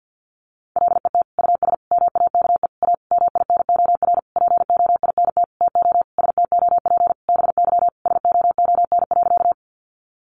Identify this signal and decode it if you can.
Morse code, heard as a keyed tone, about 48 wpm, 700 Hz. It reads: LET US MAKE A MINOR POINT TO STOP BY SOON.
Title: Sentences from Top 1500 Words 50wpm